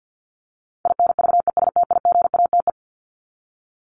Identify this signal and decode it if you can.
Morse code, heard as a keyed tone, about 50 wpm, 700 Hz. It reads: INVESTIGATE
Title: Top 1751-2000 Words 50wpm